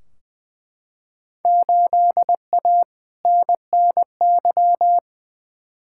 Morse code, heard as a keyed tone, about 20 wpm, 700 Hz. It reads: 8A NNY
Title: Field Day Exchanges Encore - 20wpm